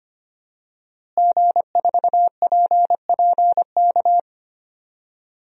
Morse code, heard as a keyed tone, about 25 wpm, 700 Hz. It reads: G4PPK
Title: DXCC-100 Call Signs - 5 Characters - Repeated 2x - 25wpm